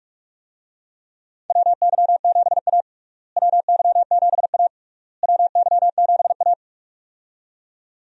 Morse code, heard as a keyed tone, about 46 wpm, 700 Hz. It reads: WY7A WY7A WY7A
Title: USA Call Signs - 4 Characters - Repeated 3x - 45wpm